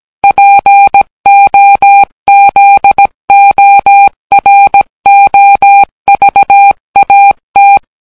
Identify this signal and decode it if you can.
Morse code, heard as a keyed tone, about 18 wpm, 800 Hz. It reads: POZOROVAT